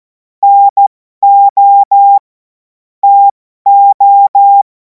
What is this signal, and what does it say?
Morse code, heard as a keyed tone, about 14 wpm, spaced as for 12 wpm, 800 Hz. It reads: NO TO
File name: Morsekurs / Witze